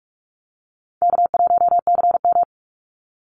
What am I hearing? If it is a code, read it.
Morse code, heard as a keyed tone, about 44 wpm, 700 Hz. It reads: K1CM
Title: NA SOTA Chaser Callsigns 45wpm